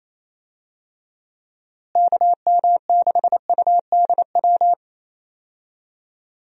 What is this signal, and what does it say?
Morse code, heard as a keyed tone, about 27 wpm, 700 Hz. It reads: KM6UDW
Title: USA Call Signs - 6 Characters - Repeated 3x - 28wpm